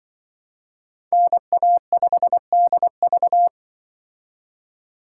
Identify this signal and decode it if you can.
Morse code, heard as a keyed tone, about 24 wpm, 700 Hz. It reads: NA5DV
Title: Speed Racing - USA 2x2 Call Signs 20wpm